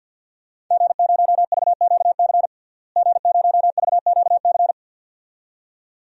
Morse code, heard as a keyed tone, about 50 wpm, 700 Hz. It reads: G0VQC G0VQC
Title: DXCC-100 Call Signs - 5 Characters - Repeated 2x - 50wpm